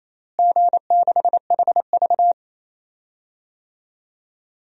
Morse code, heard as a keyed tone, about 28 wpm, 700 Hz. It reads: G6HV
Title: DXCC-100 Call Signs - 4 Characters - Repeated 3x - 28wpm